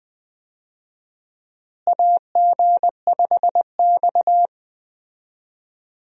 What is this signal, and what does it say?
Morse code, heard as a keyed tone, about 20 wpm, 700 Hz. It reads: AG5X
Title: USA Call Signs - 4 Characters - Repeated 3x - 20wpm